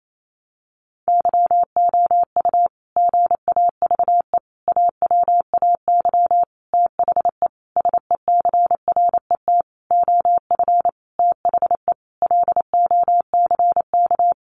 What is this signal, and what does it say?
Morse code, heard as a keyed tone, about 28 wpm, 700 Hz. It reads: YOU GAVE AWAY THE SECRET OF THE LOCK
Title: Sentences from Top 1300 Words 28wpm